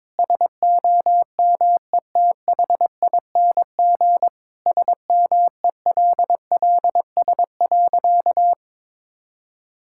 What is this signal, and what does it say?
Morse code, heard as a keyed tone, about 22 wpm, 700 Hz. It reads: SOMETHING SMELLS.
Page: morse-code-ninja-practice-22wpm